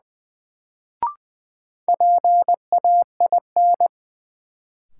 Morse code, heard as a keyed tone, about 20 wpm, 700 Hz. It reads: PAIN
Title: Sentences from Top 1600 Words - New Word Drill Alt - 20wpm